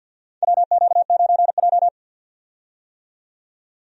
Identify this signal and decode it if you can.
Morse code, heard as a keyed tone, about 50 wpm, 700 Hz. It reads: WQ9J
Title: USA Call Signs - 4 Characters - 50wpm